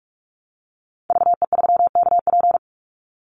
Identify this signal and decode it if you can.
Morse code, heard as a keyed tone, about 44 wpm, 700 Hz. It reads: VE3KP
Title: Call Signs - 45wpm